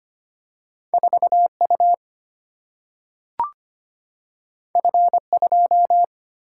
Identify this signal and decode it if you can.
Morse code, heard as a keyed tone, about 25 wpm, 700 Hz. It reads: 4 U F 2
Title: Single-Letter Single-Number 25wpm